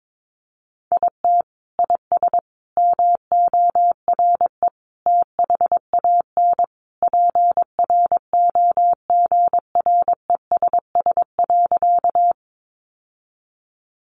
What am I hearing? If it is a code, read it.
Morse code, heard as a keyed tone, about 22 wpm, 700 Hz. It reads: IT IS MORE THAN PROGRESS.
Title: Sentences from Top 1400 Words 22wpm